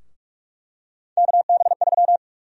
Morse code, heard as a keyed tone, about 47 wpm, 700 Hz. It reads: KB3
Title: USA Call Sign Prefixes - 45wpm